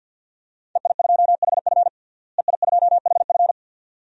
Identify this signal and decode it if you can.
Morse code, heard as a keyed tone, about 49 wpm, 700 Hz. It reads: EI2HF EI2HF
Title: DXCC-100 Call Signs - 5 Characters - Repeated 2x - 50wpm